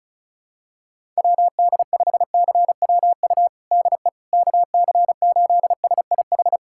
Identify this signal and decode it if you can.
Morse code, heard as a keyed tone, about 36 wpm, 700 Hz. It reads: WD5CWU DE KC8SIH
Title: USA Callsign DE Callsign 35wpm